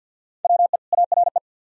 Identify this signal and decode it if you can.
Morse code, heard as a keyed tone, about 68 wpm, spaced as for 59 wpm, 700 Hz. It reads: WE ARE
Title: Speed-Racing - Sets of 2 Words 50wpm